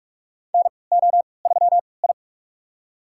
Morse code, heard as a keyed tone, about 46 wpm, spaced as for 26 wpm, 690 Hz. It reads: NO3I
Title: USA Call Signs - 4 Characters - 3x Character Spacing - 45wpm